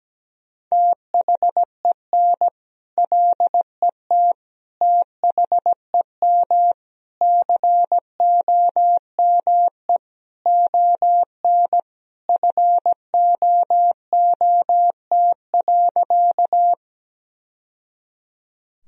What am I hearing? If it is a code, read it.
Morse code, heard as a keyed tone, about 17 wpm, 700 Hz. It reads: THEN LET THEM COME ON FOOT.
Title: Sentences from Top 1000 Words - Review Part 3 - 17wpm